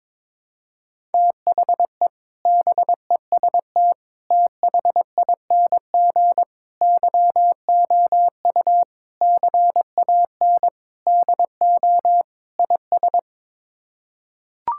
Morse code, heard as a keyed tone, about 21 wpm, 700 Hz. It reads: THE BEST THING YOU CAN DO IS
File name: practice / 22wpm / Sets of 7 Words - 22wpm